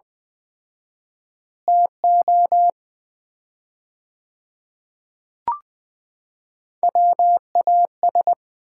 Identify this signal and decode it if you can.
Morse code, heard as a keyed tone, about 20 wpm, 700 Hz. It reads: TO WAS
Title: 4x Word-Spacing - Sets of 4 Words - 20wpm